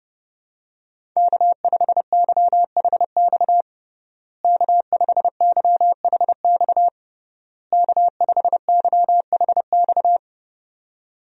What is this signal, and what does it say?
Morse code, heard as a keyed tone, about 30 wpm, 700 Hz. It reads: K5YHX K5YHX K5YHX
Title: USA Call Signs - 5 Characters - Repeated 3x - 30wpm